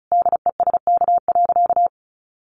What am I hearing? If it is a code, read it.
Morse code, heard as a keyed tone, about 34 wpm, 700 Hz. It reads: DESK.
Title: A Sound Of Thunder 35wpm